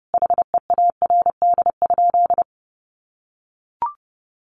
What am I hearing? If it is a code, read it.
Morse code, heard as a keyed tone, about 30 wpm, 700 Hz. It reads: HEARD?
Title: Sentences from Top 400 Words 30wpm